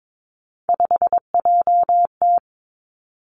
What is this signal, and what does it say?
Morse code, heard as a keyed tone, about 22 wpm, 700 Hz. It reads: 5JT
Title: Three Challenging Letter-Number Combinations 22wpm